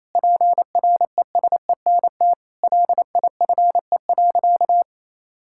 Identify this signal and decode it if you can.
Morse code, heard as a keyed tone, about 28 wpm, 700 Hz. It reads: PRESENT LIFE.
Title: Sentences from Top 1000 Words - Review Part 6 - 28wpm